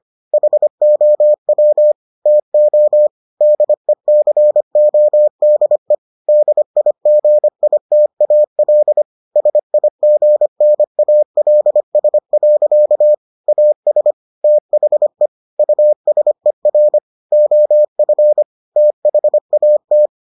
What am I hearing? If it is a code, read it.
Morse code, heard as a keyed tone, about 25 wpm, 600 Hz. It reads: HOW TO DECODE DIGITAL SIGNALS. AS THE USER OF THAT